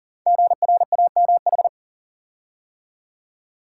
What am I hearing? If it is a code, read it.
Morse code, heard as a keyed tone, about 38 wpm, 690 Hz. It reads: GRAMH
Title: Code Groups - 5 Letters - 40wpm